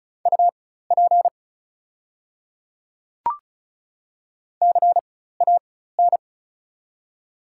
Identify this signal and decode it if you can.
Morse code, heard as a keyed tone, about 36 wpm, spaced as for 16 wpm, 700 Hz. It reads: UP CAN
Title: 4x Character Spacing - Top 100 Words - 35wpm